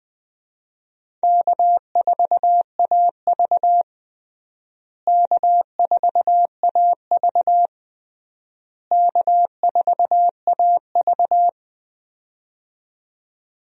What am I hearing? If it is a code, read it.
Morse code, heard as a keyed tone, about 20 wpm, 700 Hz. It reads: K4AV K4AV K4AV
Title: USA Call Signs - 4 Characters - Repeated 3x - 20wpm